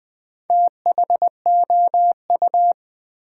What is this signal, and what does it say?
Morse code, heard as a keyed tone, about 20 wpm, 700 Hz. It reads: THOU
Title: Build A Word - 2 Letters At A Time - 8-Letter Words - 20wpm